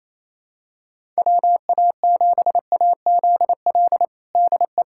Morse code, heard as a keyed tone, about 27 wpm, 700 Hz. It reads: WA7AZL DE
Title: USA Callsign DE Callsign - 6 Characters - 28wpm